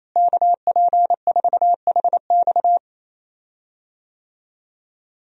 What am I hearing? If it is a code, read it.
Morse code, heard as a keyed tone, about 28 wpm, 700 Hz. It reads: KP4HX
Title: USA Call Signs - 5 Characters - Repeated 3x - 28wpm